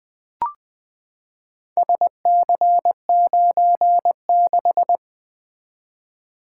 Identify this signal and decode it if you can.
Morse code, heard as a keyed tone, about 20 wpm, 700 Hz. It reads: SC96
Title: Four Letter-Number Combinations - 20wpm